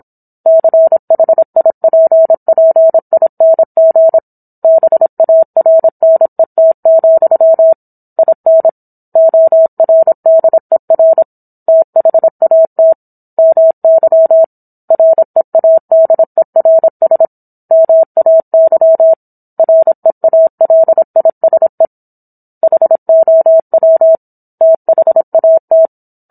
Morse code, heard as a keyed tone, about 27 wpm, 650 Hz. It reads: CHIPPING BARNET, IN ORDER THAT MY READERS MAY REALISE HOW THAT
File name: Book One / 17-Chapter 17 - 26 WPM